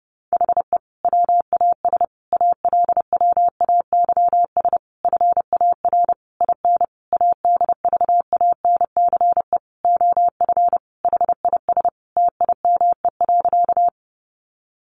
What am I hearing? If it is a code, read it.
Morse code, heard as a keyed tone, about 30 wpm, 700 Hz. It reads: HE WAS ALWAYS FAR IN ADVANCE OF HIS TIME.
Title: Sentences from Top 1400 Words 30wpm